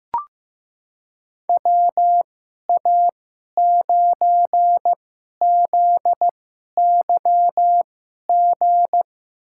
Morse code, heard as a keyed tone, about 15 wpm, spaced as for 11 wpm, 700 Hz. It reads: WA9ZYG
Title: USA Call Signs - 6 Characters - 2x Character Spacing - 15wpm